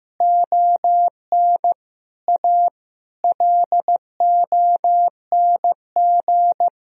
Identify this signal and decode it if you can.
Morse code, heard as a keyed tone, about 15 wpm, 700 Hz. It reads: ON A LONG
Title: Sentences from Top 1000 Words - Review Part 4 - 15wpm